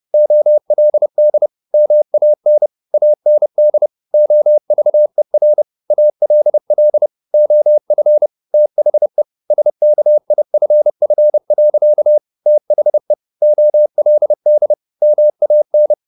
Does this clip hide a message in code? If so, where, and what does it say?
Yes: Morse code from 0.1 to 15.9 s, heard as a keyed tone, about 30 wpm, 600 Hz. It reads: OLD MAN AND OVER ALL OF THE SKIFF. THE OLD MAN